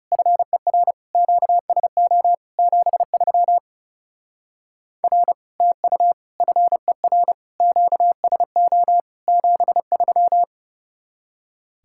Morse code, heard as a keyed tone, about 31 wpm, 700 Hz. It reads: FER QSO 73 R TU FER QSO 73
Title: Speed-Racing Common QSO Phrases - 30wpm